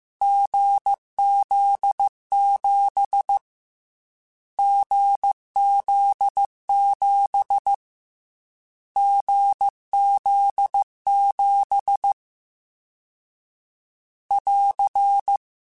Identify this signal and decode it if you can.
Morse code, heard as a keyed tone, about 15 wpm, 800 Hz. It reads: GZ7 GZ7 GZ7 <AR>
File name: Difficult Combos / Difficult Combos-mp3s / GZ GZ7 @13